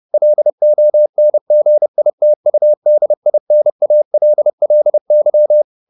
Morse code, heard as a keyed tone, about 30 wpm, 600 Hz. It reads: LONGITUDINALLY
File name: the old man and the sea / the old man and the sea 1 30wpm